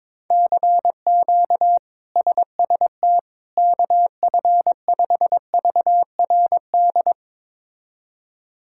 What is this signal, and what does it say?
Morse code, heard as a keyed tone, about 22 wpm, 700 Hz. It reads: CQ SST KF5VRD
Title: K1USN SST QSO Practice - 22wpm